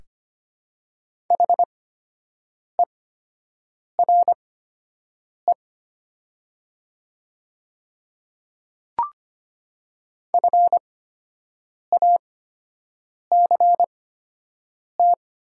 Morse code, heard as a keyed tone, about 25 wpm, spaced as for 7 wpm, 700 Hz. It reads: HERE FACT
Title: 8x Character Spacing - Top 500 Words - 25wpm